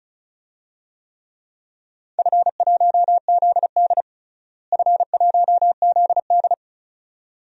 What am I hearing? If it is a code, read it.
Morse code, heard as a keyed tone, about 34 wpm, 700 Hz. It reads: F1ZD F1ZD
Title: DXCC-100 Call Signs - 4 Characters - Repeated 2x - 35wpm